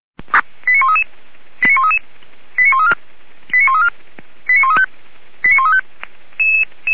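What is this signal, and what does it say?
Five-tone selective call (ZVEI), heard as quick runs of five tones, tones of about 80 ms. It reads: 89133 89133 89134 89134 89135 89135
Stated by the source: ZVEI five-tone selective call received by SDR, recorded by the fire brigade Feuerwehr Orlamünde (audio.f85360.feuerwehr-orlamuende.org).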